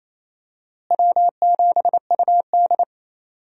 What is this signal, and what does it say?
Morse code, heard as a keyed tone, about 28 wpm, 700 Hz. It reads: W7UD
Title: USA Call Signs - 4 Characters - Repeated 2x - 28wpm